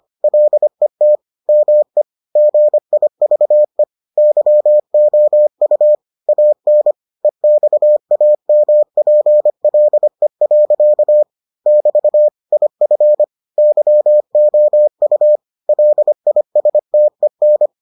Morse code, heard as a keyed tone, about 25 wpm, 600 Hz. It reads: LET ME GIVE YOU AN EXAMPLE. <BT> IF YOU LISTEN